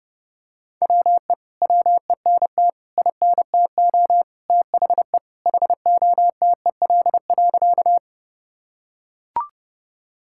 Morse code, heard as a keyed tone, about 32 wpm, 700 Hz. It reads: WE WENT INTO THE HOTEL.
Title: Sentences from Top 1600 Words 30wpm